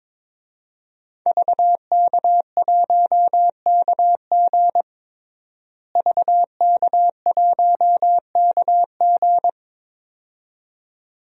Morse code, heard as a keyed tone, about 22 wpm, 700 Hz. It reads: VK1KG VK1KG
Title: DXCC-100 Call Signs - 5 Characters - Repeated 2x - 22wpm